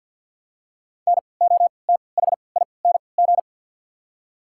Morse code, heard as a keyed tone, about 49 wpm, spaced as for 28 wpm, 690 Hz. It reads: NOTHING
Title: 3x Character Spacing - Top 500 Words - 50wpm